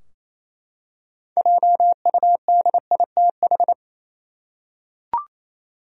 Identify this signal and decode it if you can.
Morse code, heard as a keyed tone, about 28 wpm, 700 Hz. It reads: JUDITH
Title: USA Common Names - 28wpm